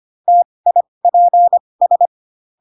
Morse code, heard as a keyed tone, about 25 wpm, spaced as for 20 wpm, 700 Hz. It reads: TIPS